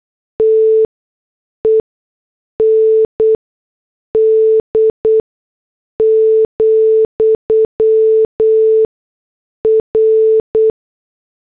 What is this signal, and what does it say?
Morse code, heard as a keyed tone, about 8 wpm, spaced as for 6 wpm, 440 Hz. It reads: TEND, R